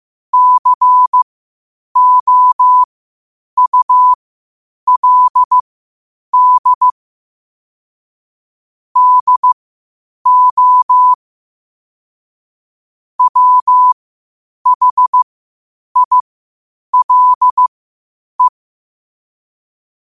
Morse code, heard as a keyed tone, about 15 wpm, spaced as for 8 wpm, 1000 Hz. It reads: COULD DO WHILE
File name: qotdfiles / 8wpm